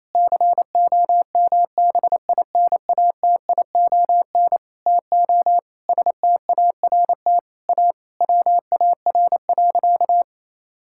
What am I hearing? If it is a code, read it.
Morse code, heard as a keyed tone, about 28 wpm, 700 Hz. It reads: COMBINATION TO START A WAR.
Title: Sentences from Top 1000 Words - Review Part 6 - 28wpm